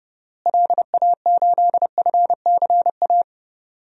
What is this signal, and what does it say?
Morse code, heard as a keyed tone, about 31 wpm, 700 Hz. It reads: LA8FCA
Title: DXCC-100 Call Signs - 6 Characters - Repeated 2x - 30wpm